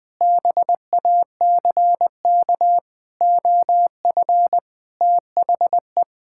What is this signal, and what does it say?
Morse code, heard as a keyed tone, about 20 wpm, 700 Hz. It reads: BACK OF THE
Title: Sentences with 4-Word Phrases - 20wpm